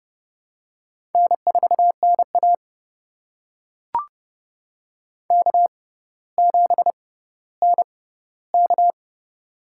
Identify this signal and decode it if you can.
Morse code, heard as a keyed tone, about 30 wpm, spaced as for 15 wpm, 700 Hz. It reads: N4NA K7NK
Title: USA Call Signs - 4 Characters - 6x Character Spacing - 30wpm